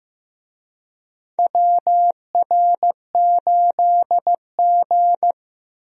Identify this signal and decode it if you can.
Morse code, heard as a keyed tone, about 15 wpm, 700 Hz. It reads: WR8G
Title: USA Call Signs - 4 Characters - Repeated 2x - 15wpm